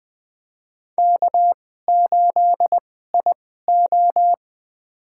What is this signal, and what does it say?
Morse code, heard as a keyed tone, about 20 wpm, spaced as for 14 wpm, 700 Hz. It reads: K8IO
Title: USA Call Signs - 4 Characters - 2x Character Spacing - 20wpm